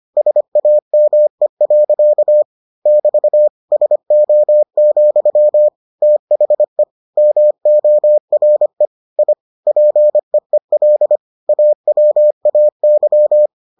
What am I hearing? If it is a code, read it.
Morse code, heard as a keyed tone, about 25 wpm, 600 Hz. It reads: SAME. <BT> SO, THE MORE I PEEL AWAY